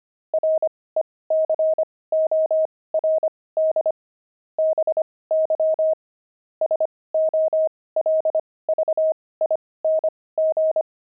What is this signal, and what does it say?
Morse code, heard as a keyed tone, about 25 wpm, spaced as for 18 wpm, 630 Hz. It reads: RECORD BY SOLVING
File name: pangram-fs-18-cs-25-puzzles